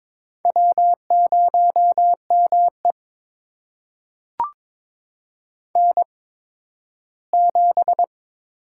Morse code, heard as a keyed tone, about 21 wpm, spaced as for 13 wpm, 700 Hz. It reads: W0ME N7
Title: USA Call Signs - 8x Character Spacing - 22wpm